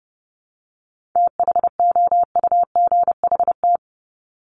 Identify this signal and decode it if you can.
Morse code, heard as a keyed tone, about 30 wpm, 700 Hz. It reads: THOUGHT
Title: Top 500 Words 30wpm